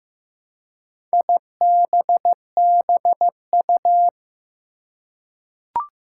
Morse code, heard as a keyed tone, about 15 wpm, 700 Hz. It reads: IBBU
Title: Four Challenging Letter-Number Combinations 15wpm